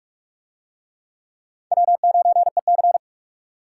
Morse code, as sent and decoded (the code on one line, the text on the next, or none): .-- ----. -.-.
W9EC